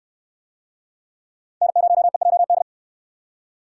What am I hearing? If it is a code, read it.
Morse code, heard as a keyed tone, about 64 wpm, 700 Hz. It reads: N0EJD